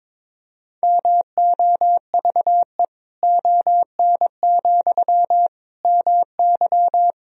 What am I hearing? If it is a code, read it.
Morse code, heard as a keyed tone, about 22 wpm, 700 Hz. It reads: MOVE ON, MY
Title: Sentences from Top 500 Words and Abbreviations 22wpm